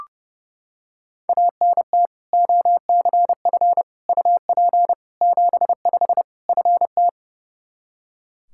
Morse code, heard as a keyed tone, about 32 wpm, 700 Hz. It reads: ANT OCF UP 75 FT
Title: Practice QSO - Exchange 2 - Info - 30wpm